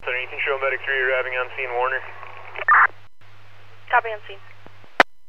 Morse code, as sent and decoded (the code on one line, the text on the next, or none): none